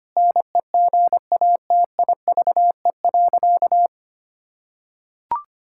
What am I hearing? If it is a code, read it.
Morse code, heard as a keyed tone, about 25 wpm, 700 Hz. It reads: NEGATIVE.
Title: Sentences from Top 1000 Words - Review Part 6 - 25wpm